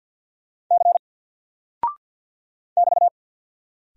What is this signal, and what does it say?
Morse code, heard as a keyed tone, about 48 wpm, 700 Hz. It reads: C <BT>
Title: Single Character (Letter:Number:Punctuation:Prosign) Alt 50wpm